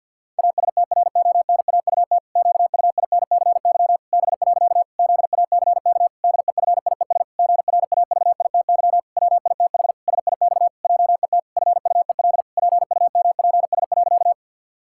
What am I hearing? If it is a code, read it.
Morse code, heard as a keyed tone, about 50 wpm, 700 Hz. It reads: ASTRONAUT QUINCY B. ZACK DEFIES GRAVITY WITH SIX JET FUEL PUMPS.
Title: Pangram Challenge v2 - 50wpm